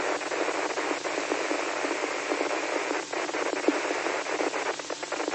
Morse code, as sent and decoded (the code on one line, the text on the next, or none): none